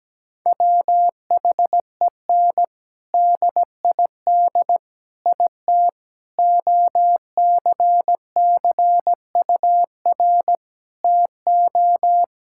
Morse code, heard as a keyed tone, about 17 wpm, 700 Hz. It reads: WHEN DID IT OCCUR TO
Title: Sentences from Top 1000 Words - Review Part 6 - 17wpm